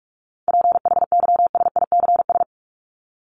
Morse code, heard as a keyed tone, about 44 wpm, 700 Hz. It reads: PHYSICS
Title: Top 501-750 Words 45wpm